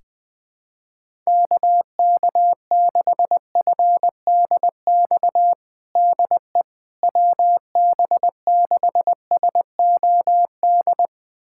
Morse code, heard as a keyed tone, about 20 wpm, 700 Hz. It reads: KK6FDX DE WB6SOD
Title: USA Callsign DE Callsign - 6 Characters - 20wpm